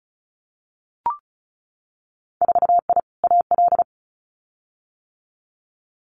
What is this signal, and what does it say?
Morse code, heard as a keyed tone, about 36 wpm, 700 Hz. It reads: 4I AL
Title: Winter Field Day Exchanges Encore 35wpm